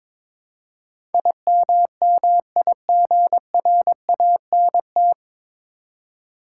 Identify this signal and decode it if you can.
Morse code, heard as a keyed tone, about 22 wpm, 700 Hz. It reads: IMMIGRANT
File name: 9 Letter Words - First 3 Letters Given - 22wpm